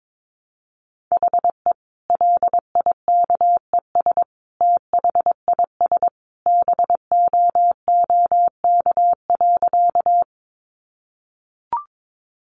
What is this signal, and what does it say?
Morse code, heard as a keyed tone, about 22 wpm, 700 Hz. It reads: HE LIKES THIS BOOK.
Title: Additional Easy Phrases 22wpm